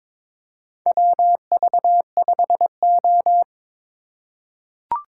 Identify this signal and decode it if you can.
Morse code, heard as a keyed tone, about 22 wpm, 700 Hz. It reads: WV5O
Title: USA Call Signs - 4 Characters - 3x Character Spacing - 22wpm